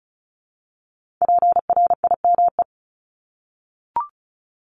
Morse code, heard as a keyed tone, about 36 wpm, 700 Hz. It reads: PRIME
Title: Multi-letter Phonemes 35wpm